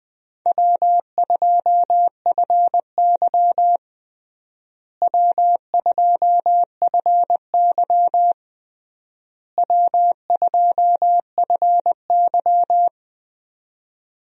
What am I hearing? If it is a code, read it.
Morse code, heard as a keyed tone, about 20 wpm, 700 Hz. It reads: W2FY W2FY W2FY
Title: USA Call Signs - 4 Characters - Repeated 3x - 20wpm